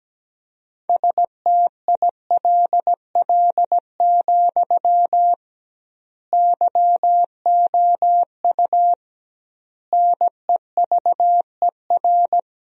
Morse code, heard as a keyed tone, about 17 wpm, 700 Hz. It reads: STILL, YOU NEVER
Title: 2x Word-Spacing - Sentences from Top 300 Words - 17wpm